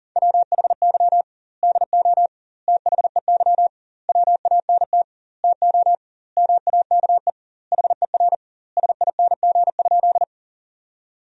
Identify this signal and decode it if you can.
Morse code, heard as a keyed tone, about 40 wpm, 700 Hz. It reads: WHY DO THEY WANT TO MAKE HER SING?
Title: 2x Word-Spacing - Sentences from Top 400 Words - 40wpm